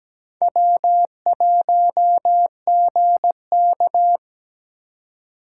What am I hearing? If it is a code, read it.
Morse code, heard as a keyed tone, about 20 wpm, 700 Hz. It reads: W1GK
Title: Speed Racing - USA 1x2 Call Signs 17wpm